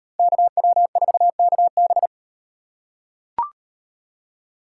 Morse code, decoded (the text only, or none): KW4KB